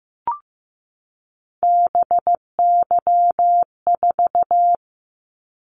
Morse code, heard as a keyed tone, about 15 wpm, 700 Hz. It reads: BY4
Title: Callsign Prefixes for Top 100 DXCC - 15wpm